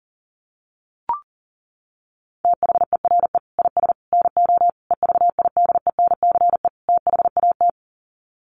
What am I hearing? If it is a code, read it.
Morse code, heard as a keyed tone, about 40 wpm, 700 Hz. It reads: THERE IS NO EVIDENCE THAT
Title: Sets of 5 Words Encore 40wpm